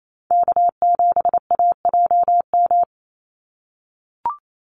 Morse code, heard as a keyed tone, about 28 wpm, 700 Hz. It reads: K7AJM
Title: Call Signs - 28wpm